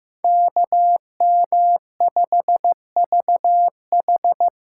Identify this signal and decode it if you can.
Morse code, heard as a keyed tone, about 15 wpm, 700 Hz. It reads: KM5VH
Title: USA Call Signs - 5 Characters - 4x Character Spacing - 15wpm